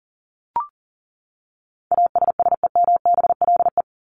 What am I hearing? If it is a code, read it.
Morse code, heard as a keyed tone, about 40 wpm, 700 Hz. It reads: ASSEMBLE
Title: LY Suffix - Top 100 Words with Base Words 40wpm